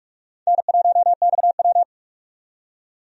Morse code, heard as a keyed tone, about 46 wpm, 700 Hz. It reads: N1XW
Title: USA Call Signs - 4 Characters - 45wpm